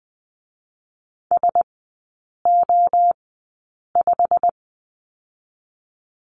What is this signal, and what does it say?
Morse code, heard as a keyed tone, about 20 wpm, spaced as for 8 wpm, 700 Hz. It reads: SO5
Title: 087 Lesson 15e - Number 5 - Three Characters 20wpm